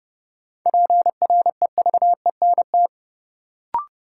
Morse code, heard as a keyed tone, about 31 wpm, 700 Hz. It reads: PREVENT
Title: Sentences from Top 1100 Words - New Word Drill Alt - 30wpm